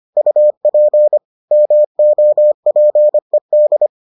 Morse code, heard as a keyed tone, about 25 wpm, 600 Hz. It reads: UP MOPED